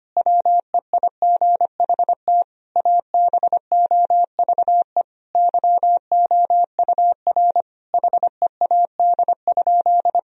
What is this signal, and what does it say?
Morse code, heard as a keyed tone, about 25 wpm, 700 Hz. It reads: WEIGHT ABOVE YOUR HEAD?
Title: Sentences from Top 1000 Words - Review Part 3 - 25wpm